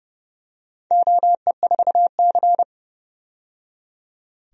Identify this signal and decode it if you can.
Morse code, heard as a keyed tone, about 30 wpm, 700 Hz. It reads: OE4C
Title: DXCC-100 Call Signs - 4 Characters - 30wpm